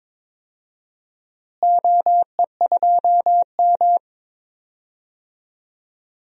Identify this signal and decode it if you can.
Morse code, heard as a keyed tone, about 22 wpm, 700 Hz. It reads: OE2M
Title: DXCC-100 Call Signs - 4 Characters - Repeated 3x - 22wpm